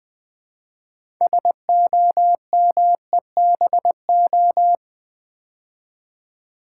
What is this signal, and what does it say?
Morse code, heard as a keyed tone, about 20 wpm, 700 Hz. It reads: SOMEBO
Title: Build A Word - 2 Letters At A Time - 8-Letter Words - 20wpm